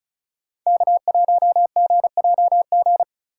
Morse code, heard as a keyed tone, about 34 wpm, 700 Hz. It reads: K1GJG
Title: USA Call Signs - 2x Character Spacing - 35wpm